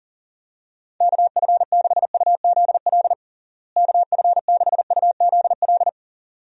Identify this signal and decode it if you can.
Morse code, heard as a keyed tone, about 37 wpm, 700 Hz. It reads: KF6UZL KF6UZL
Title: USA Call Signs - 6 Characters - Repeated 2x - 40wpm